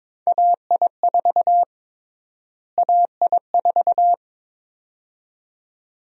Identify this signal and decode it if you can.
Morse code, heard as a keyed tone, about 22 wpm, 700 Hz. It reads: AI4 AI4
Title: USA Call Sign Prefixes - Repeated 2x - 22wpm